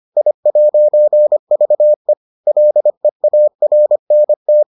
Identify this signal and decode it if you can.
Morse code, heard as a keyed tone, about 25 wpm, 600 Hz. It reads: I'VE LEARNT